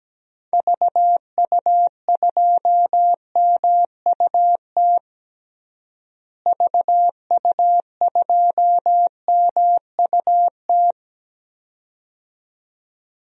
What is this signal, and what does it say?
Morse code, heard as a keyed tone, about 17 wpm, 700 Hz. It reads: VU2MUT VU2MUT
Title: DXCC-100 Call Signs - 6 Characters - Repeated 2x - 17wpm